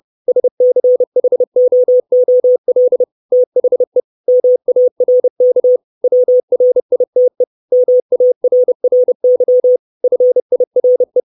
Morse code, heard as a keyed tone, about 30 wpm, 500 Hz. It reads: SCHOOL THE MARK WRITE MARRY FIRE
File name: 632 Words 30WPM